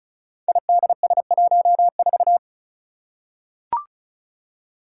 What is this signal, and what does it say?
Morse code, heard as a keyed tone, about 36 wpm, 700 Hz. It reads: IDS14
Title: Five Challenging Letter-Number Combinations 35wpm